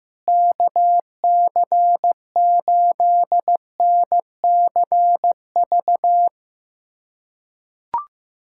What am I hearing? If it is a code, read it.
Morse code, heard as a keyed tone, about 15 wpm, 700 Hz. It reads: KC8NCV
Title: USA Callsign DE Callsign - 6 Characters - 15wpm